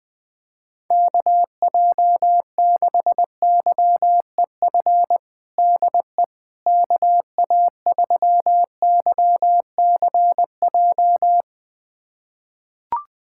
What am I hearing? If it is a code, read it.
Morse code, heard as a keyed tone, about 20 wpm, 700 Hz. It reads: KJ6YEF DE KA3YCJ
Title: USA Callsign DE Callsign 20wpm